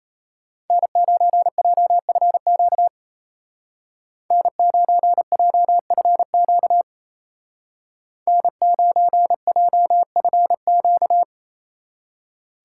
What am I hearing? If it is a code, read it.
Morse code, heard as a keyed tone, about 34 wpm, 700 Hz. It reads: N9JFQ N9JFQ N9JFQ
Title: Speed Racing - USA 1x3 Call Signs 28wpm